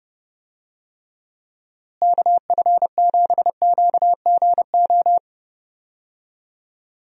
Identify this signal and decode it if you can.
Morse code, heard as a keyed tone, about 30 wpm, 700 Hz. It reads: KF7QGO
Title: USA Call Signs - 6 Characters - Repeated 2x - 30wpm